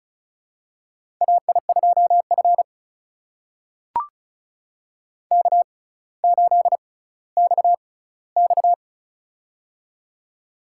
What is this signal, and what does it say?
Morse code, heard as a keyed tone, about 36 wpm, spaced as for 18 wpm, 700 Hz. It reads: AI2F K8XX
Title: USA Call Signs - 4 Characters - 6x Character Spacing - 35wpm